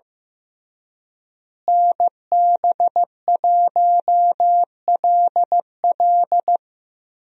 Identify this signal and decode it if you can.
Morse code, heard as a keyed tone, about 15 wpm, 700 Hz. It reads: NB1LL
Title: USA Call Signs - 5 Characters - Repeated 3x - 15wpm